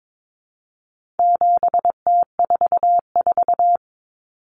Morse code, heard as a keyed tone, about 22 wpm, 700 Hz. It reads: 7T44
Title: Common Cut Numbers - 4-Digit 22wpm